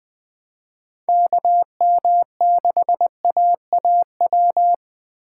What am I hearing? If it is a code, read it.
Morse code, heard as a keyed tone, about 20 wpm, 700 Hz. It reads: KM6AAW
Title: Speed Racing - USA 2x3 Call Signs 17wpm